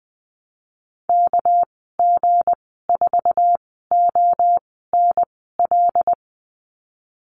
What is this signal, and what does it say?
Morse code, heard as a keyed tone, about 20 wpm, spaced as for 15 wpm, 700 Hz. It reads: KG4ONL
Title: 200 Lesson 32g - Letter k - Call Signs 20wpm